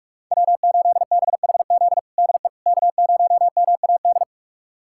Morse code, heard as a keyed tone, about 46 wpm, 700 Hz. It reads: W8BHZ DE K0MAD
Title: USA Callsign DE Callsign - 5 Characters - 45wpm